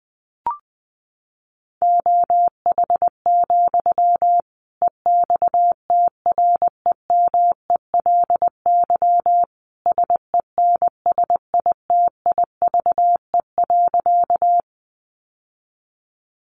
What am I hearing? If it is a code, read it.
Morse code, heard as a keyed tone, about 19 wpm, 700 Hz. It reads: OH, EXTREMELY SENSITIVE.
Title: Sentences from Top 1400 Words 20wpm